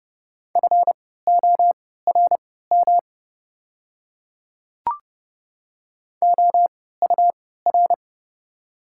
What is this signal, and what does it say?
Morse code, heard as a keyed tone, about 30 wpm, spaced as for 17 wpm, 700 Hz. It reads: FORM OUR
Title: 3x Character Spacing - Top 200 Words - 30wpm